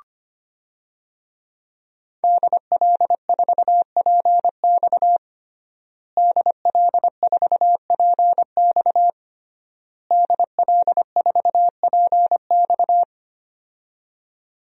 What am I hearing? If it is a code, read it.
Morse code, heard as a keyed tone, about 25 wpm, 700 Hz. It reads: DL4PX DL4PX DL4PX
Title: DXCC-100 Call Signs - 5 Characters - Repeated 3x - 25wpm